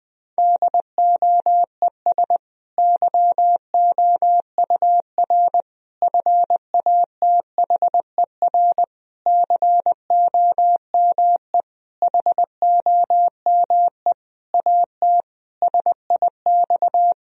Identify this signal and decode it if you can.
Morse code, heard as a keyed tone, about 20 wpm, 700 Hz. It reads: DOES YOUR FATHER COME HOME AT SIX
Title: Sentences from Top 1000 Words - Review Part 2 - 20wpm